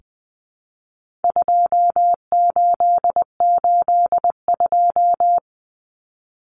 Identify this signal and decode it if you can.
Morse code, heard as a keyed tone, about 20 wpm, 700 Hz. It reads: 2882
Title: Common Cut Numbers - 4-Digit 20wpm